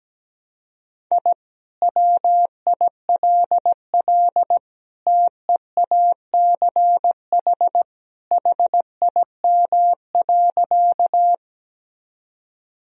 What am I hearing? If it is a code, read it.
Morse code, heard as a keyed tone, about 17 wpm, 700 Hz. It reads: I WILL TEACH HIM.
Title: Sentences from Top 1000 Words - Review Part 2 - 17wpm